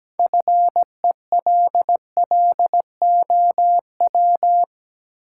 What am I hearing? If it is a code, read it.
Morse code, heard as a keyed tone, about 17 wpm, 700 Hz. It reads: FELLOW
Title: Sets of 5 Words Encore - New Word Drill - 17wpm